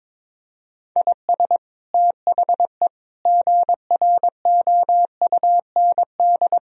Morse code, heard as a keyed tone, about 22 wpm, 700 Hz. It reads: IS THE GROUND
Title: Sentences from Top 1000 Words - Review Part 4 - 22wpm